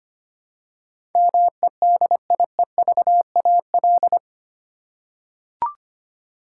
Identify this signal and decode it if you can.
Morse code, heard as a keyed tone, about 25 wpm, 690 Hz. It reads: MEDIEVAL
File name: 8 Letter Words - First 3 Letters Given - 25wpm